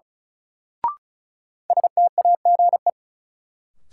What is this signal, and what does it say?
Morse code, heard as a keyed tone, about 34 wpm, 700 Hz. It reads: STAGE
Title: Sentences from Top 1200 Words - New Word Drill - 35wpm